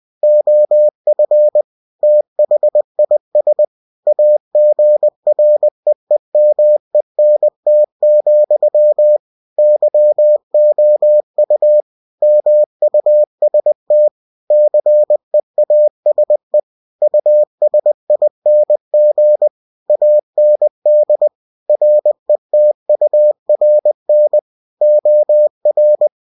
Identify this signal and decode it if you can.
Morse code, heard as a keyed tone, about 20 wpm, 600 Hz. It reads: OF THIS AGREEMENT, YOU MUST CEASE USING AND RETURN OR
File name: cw-books / winnie-the-pooh / winnie-20wpm-page-0113